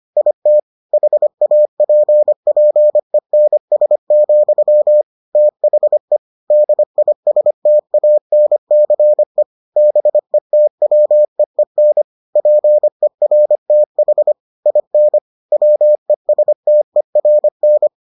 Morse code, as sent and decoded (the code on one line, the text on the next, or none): .. - / .... .- .--. .--. -. ... --..-- / - .... / -.. .. ... - .- -. -.-. / -... - .-- . -. / .--. .-. - .... / .. -. / .-- . ... - . .-. -.
IT HAPPENS, THE DISTANCE BETWEEN PERTH IN WESTERN